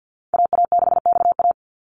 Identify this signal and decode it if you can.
Morse code, heard as a keyed tone, about 50 wpm, 700 Hz. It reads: AA6XA
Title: Call Signs - 50wpm